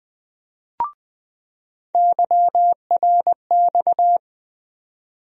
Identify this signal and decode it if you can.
Morse code, heard as a keyed tone, about 20 wpm, 700 Hz. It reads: YRX
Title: USA Call Sign Postfixes - 3 Letters 20wpm